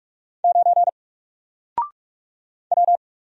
Morse code, heard as a keyed tone, about 44 wpm, 700 Hz. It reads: 9 W
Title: Single Letter-Number Alt 45wpm